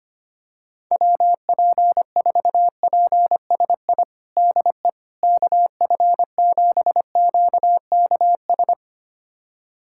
Morse code, heard as a keyed tone, about 25 wpm, 700 Hz. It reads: WP4PSI DE KF7QKS
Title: USA Callsign DE Callsign - 6 Characters - 25wpm